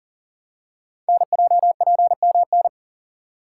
Morse code, heard as a keyed tone, about 42 wpm, 700 Hz. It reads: NJPMN
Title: Code Groups - 5 Letters - 40wpm